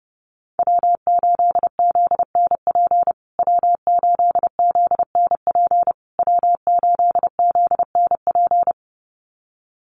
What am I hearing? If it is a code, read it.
Morse code, heard as a keyed tone, about 30 wpm, 700 Hz. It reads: W8ZNP W8ZNP W8ZNP
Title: USA Call Signs Repeated 3x 30wpm